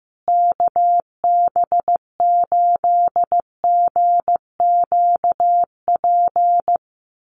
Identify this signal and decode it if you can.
Morse code, heard as a keyed tone, about 15 wpm, 700 Hz. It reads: KB8GQP
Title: USA Call Signs Repeated 3x 15wpm